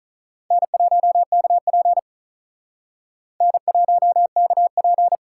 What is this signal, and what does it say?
Morse code, heard as a keyed tone, about 44 wpm, 700 Hz. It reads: N1KP N1KP
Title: Speed Racing - USA 1x2 Call Signs 35wpm